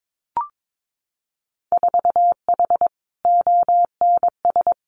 Morse code, heard as a keyed tone, about 22 wpm, 700 Hz. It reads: 4H ONS
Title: Winter Field Day Exchanges Encore 22wpm